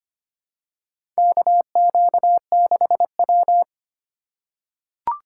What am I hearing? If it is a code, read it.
Morse code, heard as a keyed tone, about 25 wpm, 700 Hz. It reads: KQ6W
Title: USA Call Signs - 2x Character Spacing - 25wpm